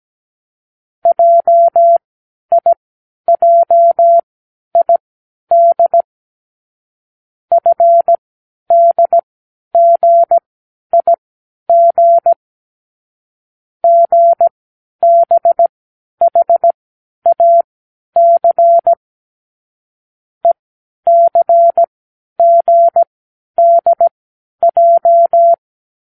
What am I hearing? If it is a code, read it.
Morse code, heard as a keyed tone, about 17 wpm, spaced as for 10 wpm, 680 Hz. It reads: JIJID FDGIG GBHAC ECGDJ
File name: k7qo / Lesson15(A-J)